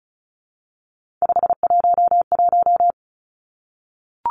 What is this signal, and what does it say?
Morse code, heard as a keyed tone, about 36 wpm, 700 Hz. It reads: 511
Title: Three Challenging Letter-Number Combinations 35wpm